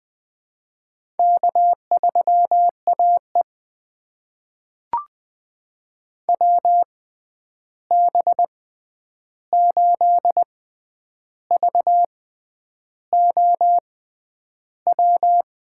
Morse code, heard as a keyed tone, about 21 wpm, spaced as for 9 wpm, 700 Hz. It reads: K3AE WB8VOW
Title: USA Call Signs - 6x Character Spacing - 20wpm